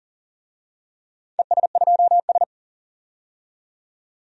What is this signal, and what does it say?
Morse code, heard as a keyed tone, about 40 wpm, 700 Hz. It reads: ES2S
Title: DXCC-100 Call Signs - 4 Characters - 40wpm